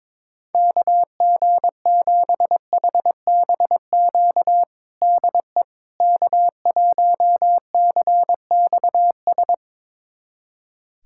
Morse code, heard as a keyed tone, about 22 wpm, 700 Hz. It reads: KG7HBQ DE K1CXS
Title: USA Callsign DE Callsign 22wpm